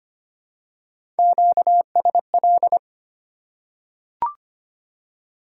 Morse code, heard as a keyed tone, about 25 wpm, 690 Hz. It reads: QSL
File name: Top 90 QSO Elements - 4x Character Spacing - 25wpm